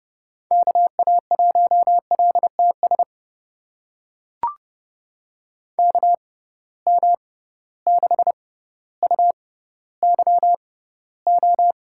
Morse code, heard as a keyed tone, about 29 wpm, spaced as for 15 wpm, 700 Hz. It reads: KA1LTS KM6UYO
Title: USA Call Signs - 6 Characters - 6x Character Spacing - 30wpm